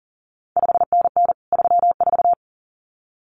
Morse code, heard as a keyed tone, about 38 wpm, 700 Hz. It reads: 5NN 34
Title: CQ WW DX Contest Exchange Encore - 40wpm